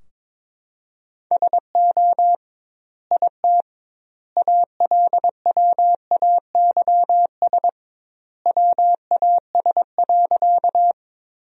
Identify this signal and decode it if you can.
Morse code, heard as a keyed tone, about 21 wpm, 700 Hz. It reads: SO IT ALWAYS WAS.
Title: 2x Word-Spacing - Sentences from Top 300 Words - 22wpm